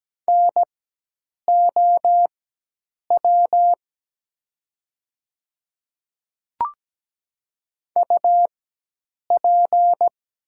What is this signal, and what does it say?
Morse code, heard as a keyed tone, about 17 wpm, spaced as for 8 wpm, 700 Hz. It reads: NOW UP
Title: 4x Character Spacing - Top 100 Words - 17wpm